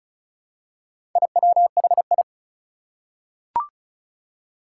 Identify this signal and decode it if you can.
Morse code, heard as a keyed tone, about 36 wpm, 700 Hz. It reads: IWHI
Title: Four Challenging Letter-Number Combinations 35wpm